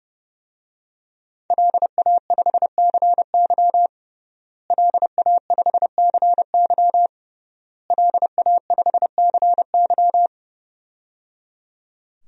Morse code, heard as a keyed tone, about 30 wpm, 700 Hz. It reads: LA5CY LA5CY LA5CY
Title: DXCC-100 Call Signs - 5 Characters - Repeated 3x - 30wpm